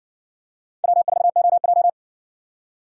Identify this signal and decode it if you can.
Morse code, heard as a keyed tone, about 58 wpm, 700 Hz. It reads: W4OJ